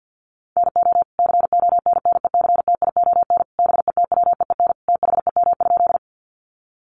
Morse code, heard as a keyed tone, about 48 wpm, 700 Hz. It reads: NO CONNECTION BETWEEN THEM?
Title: Sentences from Top 1300 Words 50wpm